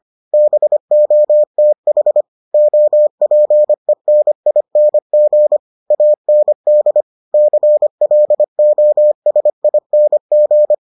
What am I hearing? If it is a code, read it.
Morse code, heard as a keyed tone, about 25 wpm, 600 Hz. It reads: BOTH OPENING AND CLOSING